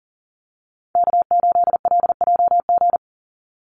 Morse code, heard as a keyed tone, about 40 wpm, 700 Hz. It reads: K8LJG
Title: NA SOTA Chaser Callsigns 40wpm